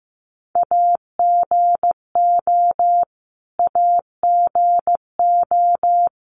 Morse code, heard as a keyed tone, about 15 wpm, 700 Hz. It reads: AGO AGO
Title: Top 500 Words Repeated 15wpm